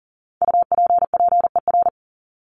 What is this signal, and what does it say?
Morse code, heard as a keyed tone, about 37 wpm, 700 Hz. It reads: UPPER
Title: Top 1751-2000 Words 40wpm